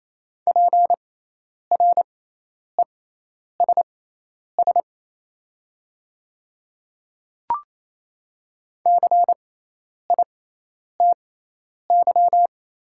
Morse code, heard as a keyed tone, about 28 wpm, spaced as for 10 wpm, 700 Hz. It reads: PRESS CITY
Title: 6x Character Spacing - Top 300 Words - 28wpm